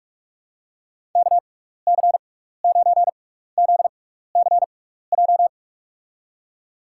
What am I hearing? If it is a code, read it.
Morse code, heard as a keyed tone, about 44 wpm, spaced as for 15 wpm, 700 Hz. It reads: KC9ZCJ
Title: USA Call Signs - 6 Characters - 6x Character Spacing - 45wpm